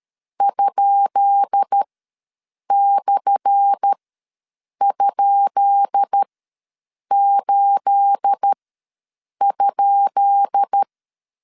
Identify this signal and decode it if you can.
Morse code, heard as a keyed tone, about 13 wpm, spaced as for 7 wpm, 800 Hz. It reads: ?/?8?
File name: LEZIONI / 10 SIGNS / 10B INT ws 12wpm 3527